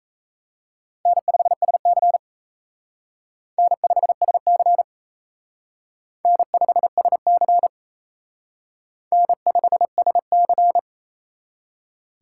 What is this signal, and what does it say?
Morse code, heard as a keyed tone, about 35 wpm, 700 Hz. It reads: N5SC N5SC N5SC N5SC
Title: Speed Racing - USA 1x2 Call Signs 28wpm